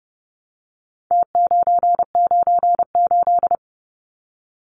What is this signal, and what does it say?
Morse code, heard as a keyed tone, about 30 wpm, 700 Hz. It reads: T998
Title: Numbers - Contest Style - 4-Digit 30wpm